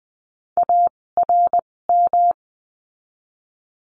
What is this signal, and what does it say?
Morse code, heard as a keyed tone, about 20 wpm, spaced as for 16 wpm, 700 Hz. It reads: ARM
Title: 256 Lesson 42a - Words 12-20wpm